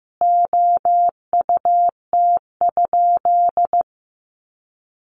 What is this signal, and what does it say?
Morse code, heard as a keyed tone, about 15 wpm, 700 Hz. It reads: OUT?
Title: Sentences from Top 1400 Words 15wpm